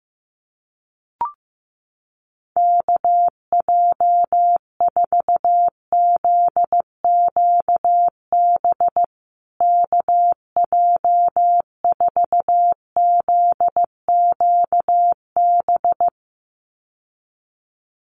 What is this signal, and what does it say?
Morse code, heard as a keyed tone, about 15 wpm, 700 Hz. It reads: KJ4ZQB KJ4ZQB
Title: USA Call Signs Repeated 2x 15wpm